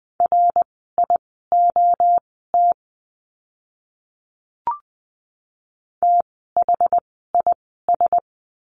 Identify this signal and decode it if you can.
Morse code, heard as a keyed tone, about 20 wpm, spaced as for 15 wpm, 700 Hz. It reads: RIOT THIS
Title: 061 Lesson 11f - Letter h - Words 20wpm